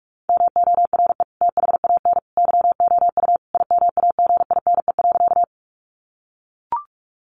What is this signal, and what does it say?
Morse code, heard as a keyed tone, about 44 wpm, 700 Hz. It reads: MORE THAN YOU IMAGINE.
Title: Sentences from Top 1400 Words 45wpm